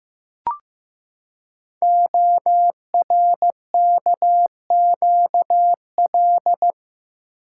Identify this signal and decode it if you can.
Morse code, heard as a keyed tone, about 15 wpm, 700 Hz. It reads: ORKQL
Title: Code Groups - 5 Letters - 15wpm